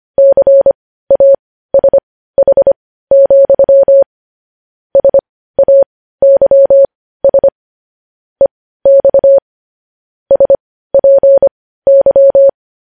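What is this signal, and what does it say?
Morse code, heard as a keyed tone, about 25 wpm, spaced as for 15 wpm, 550 Hz. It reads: CASH, SAYS EX SPY
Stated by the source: Morse code news headlines sent at 15 words per minute.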